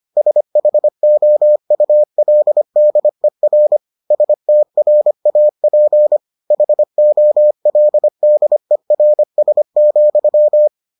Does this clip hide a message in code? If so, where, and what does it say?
Yes: Morse code from 0.2 to 10.7 s, heard as a keyed tone, about 25 wpm, 600 Hz. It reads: SHOULDER STRAP HOLDERS,